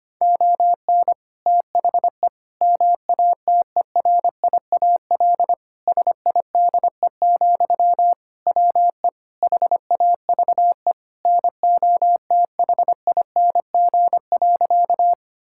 Morse code, heard as a keyed tone, about 25 wpm, 700 Hz. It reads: ON THE MATERIAL SIDE, WE HAVE NOTHING.
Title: Sentences from Top 1000 Words - Review Part 5 - 25wpm